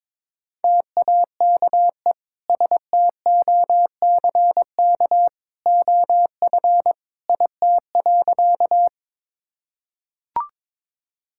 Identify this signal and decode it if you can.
Morse code, heard as a keyed tone, about 21 wpm, 700 Hz. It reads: TAKE STOCK OF IT.
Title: Sentences from Top 1000 Words - Review Part 5 - 22wpm